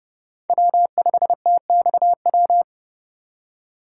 Morse code, heard as a keyed tone, about 31 wpm, 700 Hz. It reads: W5TXW
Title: USA Call Signs - 5 Characters - 6x Character Spacing - 30wpm